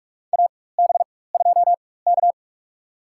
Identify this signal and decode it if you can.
Morse code, heard as a keyed tone, about 46 wpm, spaced as for 21 wpm, 700 Hz. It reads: AB2K
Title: USA Call Signs - 4 Characters - 4x Character Spacing - 45wpm